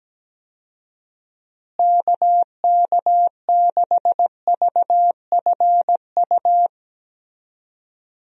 Morse code, heard as a keyed tone, about 17 wpm, 700 Hz. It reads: KK6VFU
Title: USA Call Signs - 6 Characters - Repeated 3x - 17wpm